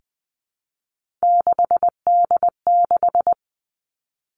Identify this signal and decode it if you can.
Morse code, heard as a keyed tone, about 20 wpm, 700 Hz. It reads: 6D6
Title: Three Challenging Letter-Number Combinations 20wpm